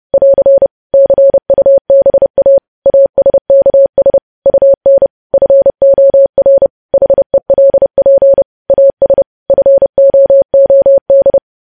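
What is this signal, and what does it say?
Morse code, heard as a keyed tone, about 30 wpm, 550 Hz. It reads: <AR> CUBA ASKS UN FOR HELP AS FOOD
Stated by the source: Morse news headlines at 30 wpm.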